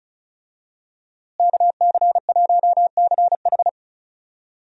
Morse code, heard as a keyed tone, about 36 wpm, 700 Hz. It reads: KC1CH
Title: USA Call Signs - 5 Characters - Repeated 3x - 35wpm